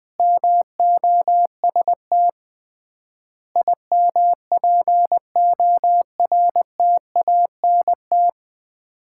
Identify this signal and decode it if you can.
Morse code, heard as a keyed tone, about 20 wpm, 700 Hz. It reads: MOST IMPORTANT
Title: 3x Word-Spacing - Sets of 4 Words - 20wpm